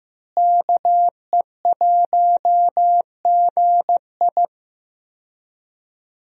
Morse code, heard as a keyed tone, about 15 wpm, 700 Hz. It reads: KE1GI
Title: USA Call Signs - 5 Characters - Repeated 2x - 15wpm